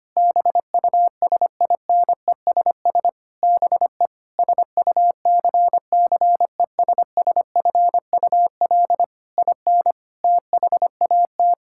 Morse code, heard as a keyed tone, about 25 wpm, 700 Hz. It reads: BUSINESS BE SUCCESSFUL IN THAT
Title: Sentences from Top 1000 Words - Review Part 5 - 25wpm